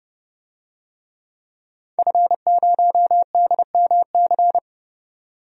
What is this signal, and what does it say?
Morse code, heard as a keyed tone, about 30 wpm, 700 Hz. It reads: F0DMC
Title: DXCC-100 Call Signs - 5 Characters - 30wpm